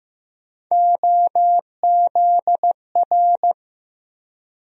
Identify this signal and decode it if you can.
Morse code, heard as a keyed tone, about 15 wpm, 700 Hz. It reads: OZR
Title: USA Call Sign Postfixes - 3 Letters 15wpm